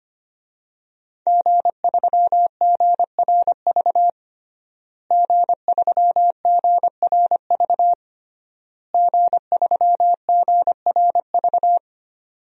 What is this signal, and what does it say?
Morse code, heard as a keyed tone, about 25 wpm, 700 Hz. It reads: G3GRV G3GRV G3GRV
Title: DXCC-100 Call Signs - 5 Characters - Repeated 3x - 25wpm